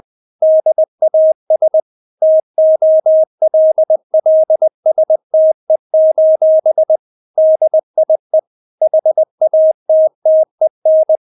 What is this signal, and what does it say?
Morse code, heard as a keyed tone, about 20 wpm, 630 Hz. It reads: DAS TOLLSTE: DIE HATTEN
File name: Witz1 20wpm